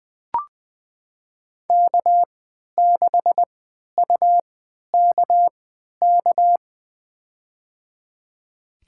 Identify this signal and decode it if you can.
Morse code, heard as a keyed tone, about 20 wpm, spaced as for 11 wpm, 700 Hz. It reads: K6UKK
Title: USA Call Signs - 5 Characters - 3x Character Spacing - 20wpm